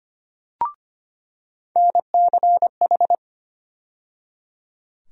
Morse code, heard as a keyed tone, about 25 wpm, 700 Hz. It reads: NCH
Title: USA Call Sign Postfixes - 3 Letters 25wpm